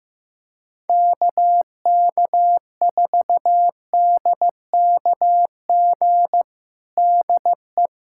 Morse code, heard as a keyed tone, about 15 wpm, 700 Hz. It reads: KK4DKG DE
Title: USA Callsign DE Callsign 15wpm